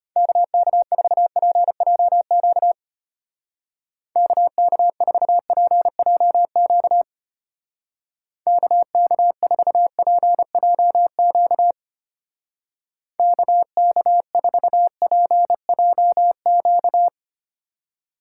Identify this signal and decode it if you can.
Morse code, heard as a keyed tone, about 31 wpm, 700 Hz. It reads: KK4PJQ KK4PJQ KK4PJQ KK4PJQ
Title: Speed Racing - USA 2x3 Call Signs 25wpm